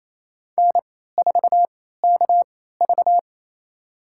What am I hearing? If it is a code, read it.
Morse code, heard as a keyed tone, about 28 wpm, spaced as for 16 wpm, 700 Hz. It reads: N4KV
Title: USA Call Signs - 4 Characters - 3x Character Spacing - 28wpm